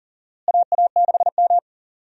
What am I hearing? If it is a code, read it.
Morse code, heard as a keyed tone, about 40 wpm, 700 Hz. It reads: AA6M
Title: USA Call Signs - 4 Characters - Repeated 3x - 40wpm